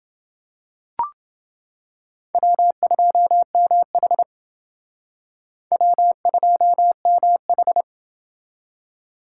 Morse code, heard as a keyed tone, about 28 wpm, 700 Hz. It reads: W2MH W2MH